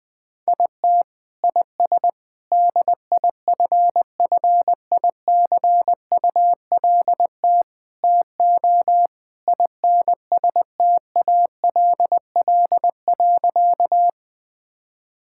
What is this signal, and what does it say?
Morse code, heard as a keyed tone, about 21 wpm, 700 Hz. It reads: IT IS DIFFICULT TO INSTALL.
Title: Sentences from Top 1600 Words 20wpm